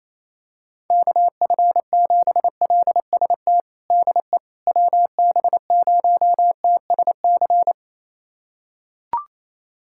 Morse code, heard as a keyed tone, about 28 wpm, 700 Hz. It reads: KF7LST DE WB0TSC
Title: USA Callsign DE Callsign 28wpm